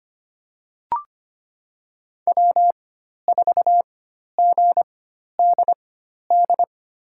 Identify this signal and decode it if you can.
Morse code, heard as a keyed tone, about 25 wpm, spaced as for 12 wpm, 700 Hz. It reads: W4GDD
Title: USA Call Signs - 4x Character Spacing - 25wpm